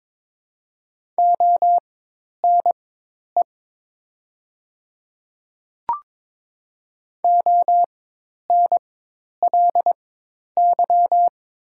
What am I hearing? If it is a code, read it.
Morse code, heard as a keyed tone, about 22 wpm, spaced as for 10 wpm, 700 Hz. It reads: ONE ONLY
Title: 4x Character Spacing - Top 200 Words - 22wpm